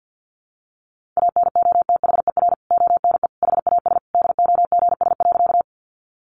Morse code, heard as a keyed tone, about 49 wpm, 700 Hz. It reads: ANOTHER ONE HAS DOGS.
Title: Additional Easy Phrases 50wpm